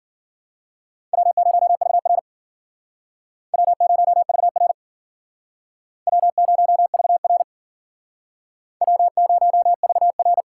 Morse code, heard as a keyed tone, about 51 wpm, 700 Hz. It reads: W0VR W0VR W0VR W0VR